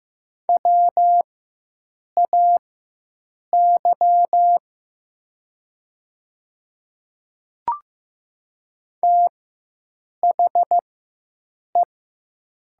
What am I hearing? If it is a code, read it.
Morse code, heard as a keyed tone, about 15 wpm, spaced as for 7 wpm, 700 Hz. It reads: WAY THE
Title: 4x Character Spacing - Top 100 Words - 15wpm